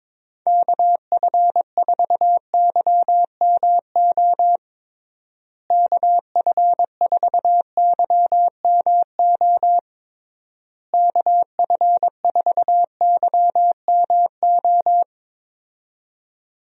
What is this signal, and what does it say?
Morse code, heard as a keyed tone, about 22 wpm, 700 Hz. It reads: KF4YMO KF4YMO KF4YMO
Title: USA Call Signs - 6 Characters - Repeated 3x - 22wpm